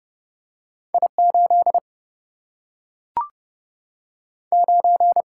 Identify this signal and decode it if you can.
Morse code, heard as a keyed tone, about 30 wpm, 700 Hz. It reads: I8 9
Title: Single-Letter Single-Number 30wpm